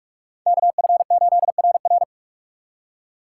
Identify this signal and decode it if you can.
Morse code, heard as a keyed tone, about 59 wpm, 700 Hz. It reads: KF8RR